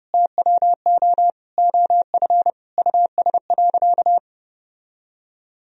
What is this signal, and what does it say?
Morse code, heard as a keyed tone, about 28 wpm, 700 Hz. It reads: TWO OF US.
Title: Sentences from Top 1000 Words - Review Part 1 - 30wpm